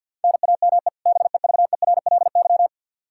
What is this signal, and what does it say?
Morse code, heard as a keyed tone, about 52 wpm, 700 Hz. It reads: NAME BEVERLY
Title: Practice QSO - Exchange 1 - Beginning - 50wpm